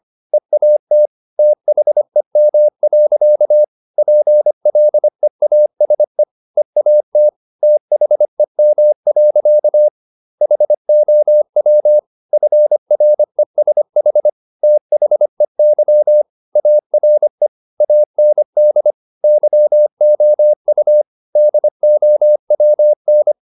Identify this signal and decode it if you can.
Morse code, heard as a keyed tone, about 25 wpm, 600 Hz. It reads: EAT THEM. PLEASE EAT THEM. HOW FRESH THEY ARE AND YOU DOWN
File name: the old man and the sea part 1 25wpm